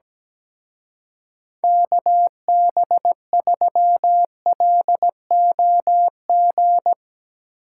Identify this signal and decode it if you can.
Morse code, heard as a keyed tone, about 17 wpm, 700 Hz. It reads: KB3LOG
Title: Speed Racing - USA 2x3 Call Signs 17wpm